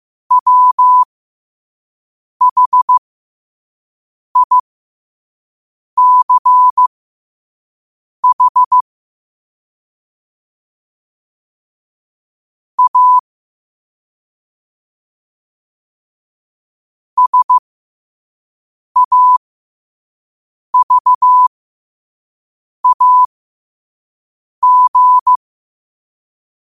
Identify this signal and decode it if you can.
Morse code, heard as a keyed tone, about 15 wpm, spaced as for 5 wpm, 1000 Hz. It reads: WHICH A SAVAG
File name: qotdfiles / 5wpm